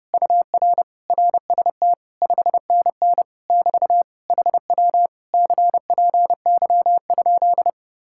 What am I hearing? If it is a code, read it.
Morse code, heard as a keyed tone, about 30 wpm, 700 Hz. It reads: UR RST 5NN <BT> HW CPY?